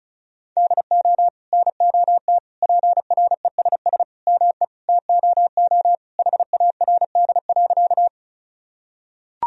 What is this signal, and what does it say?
Morse code, heard as a keyed tone, about 36 wpm, 700 Hz. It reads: DO NOT PRESS ME TOO HARD.
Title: Sentences from Top 1000 Words - Review Part 1 - 35wpm